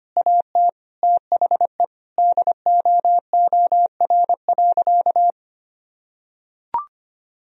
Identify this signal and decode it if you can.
Morse code, heard as a keyed tone, about 25 wpm, 700 Hz. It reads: AT THE DOOR.
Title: Sentences from Top 1000 Words - Review Part 2 - 25wpm